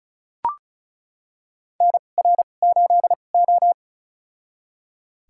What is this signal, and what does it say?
Morse code, heard as a keyed tone, about 36 wpm, spaced as for 25 wpm, 700 Hz. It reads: NR8O
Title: USA Call Signs - 4 Characters - 2x Character Spacing - 35wpm